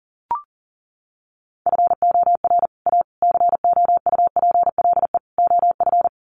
Morse code, heard as a keyed tone, about 40 wpm, 700 Hz. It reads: FOR A COUPLE OF
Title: Sets of 4 Words 40wpm